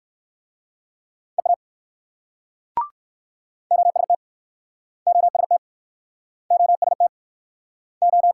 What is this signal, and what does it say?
Morse code, heard as a keyed tone, about 55 wpm, 700 Hz. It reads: EA OST OST OST O
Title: Speed-Racing Multi-letter Phonemes - 45wpm